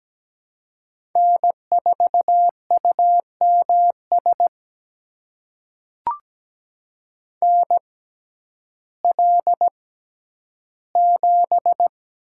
Morse code, heard as a keyed tone, about 17 wpm, spaced as for 10 wpm, 700 Hz. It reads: N4UMS NL7
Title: USA Call Signs - 5 Characters - 6x Character Spacing - 17wpm